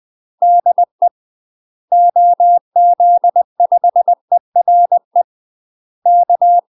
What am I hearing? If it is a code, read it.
Morse code, heard as a keyed tone, about 20 wpm, 700 Hz. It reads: DE OZ5ERE K
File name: UK MASTER QSOs TRAINING / 20WPM / UK MASTER QSOs 0070